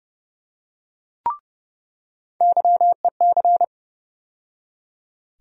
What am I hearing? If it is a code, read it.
Morse code, heard as a keyed tone, about 31 wpm, 700 Hz. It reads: YEC
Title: USA Call Sign Postfixes - 3 Letters 30wpm